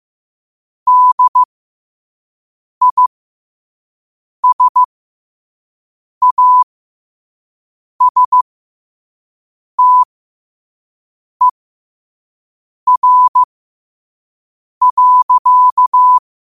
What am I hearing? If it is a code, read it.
Morse code, heard as a keyed tone, about 15 wpm, spaced as for 5 wpm, 1000 Hz. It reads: DISASTER.
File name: qotdfiles / 5wpm